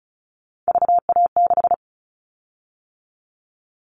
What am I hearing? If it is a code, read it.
Morse code, heard as a keyed tone, about 34 wpm, 700 Hz. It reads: VA6
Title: Callsign Prefixes for Top 50 DXCC - 35wpm